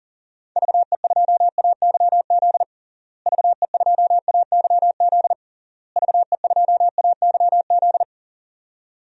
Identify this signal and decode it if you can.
Morse code, heard as a keyed tone, about 40 wpm, 700 Hz. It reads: VE2AYZ VE2AYZ VE2AYZ
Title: DXCC-100 Call Signs - 6 Characters - Repeated 3x - 40wpm